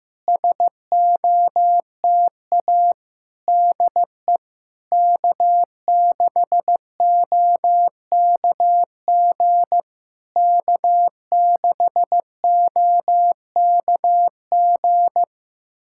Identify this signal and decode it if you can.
Morse code, heard as a keyed tone, about 15 wpm, 690 Hz. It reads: SOTA DE K6OKG K6OKG
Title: SOTA QSO Practice 15wpm